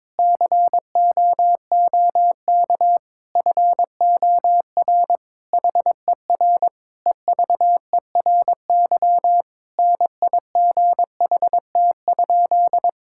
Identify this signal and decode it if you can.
Morse code, heard as a keyed tone, about 22 wpm, 700 Hz. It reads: COOK FOR HER EVERY NIGHT?
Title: Sentences from Top 1000 Words - Review Part 5 - 22wpm